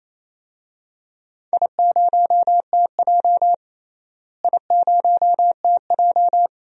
Morse code, heard as a keyed tone, about 28 wpm, 700 Hz. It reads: I0TJ I0TJ
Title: DXCC-100 Call Signs - 4 Characters - Repeated 3x - 28wpm